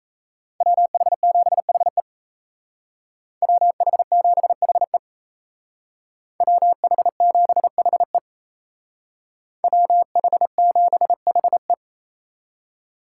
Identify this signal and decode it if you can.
Morse code, heard as a keyed tone, about 36 wpm, 700 Hz. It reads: WH7HE WH7HE WH7HE WH7HE